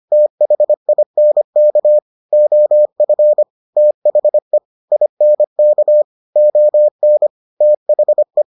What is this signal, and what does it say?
Morse code, heard as a keyed tone, about 25 wpm, 600 Hz. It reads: THINK OF THE INK ON THE